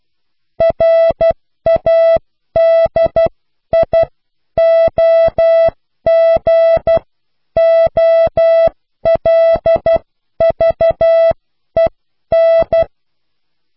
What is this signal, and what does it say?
Morse code, heard as a keyed tone, about 12 wpm, spaced as for 10 wpm, 660 Hz. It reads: RADIOGOLVEN